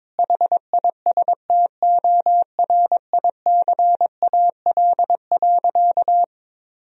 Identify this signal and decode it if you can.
Morse code, heard as a keyed tone, about 22 wpm, 700 Hz. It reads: HISTORICAL.
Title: Sentences from Top 1000 Words - Review Part 4 - 22wpm